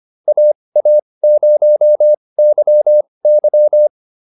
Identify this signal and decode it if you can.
Morse code, heard as a keyed tone, about 25 wpm, spaced as for 20 wpm, 600 Hz. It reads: AA0YY